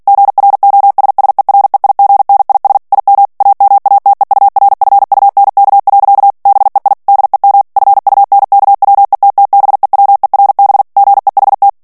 Morse code, heard as a keyed tone, about 48 wpm, 800 Hz. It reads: GROSSEREIGNIS IM AMATEURFUNK. BEI DEM FUNKWETTBEWERB GEHT
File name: morse mp3 / ALLEMAND texte clair 2 vitesse 45 mots minute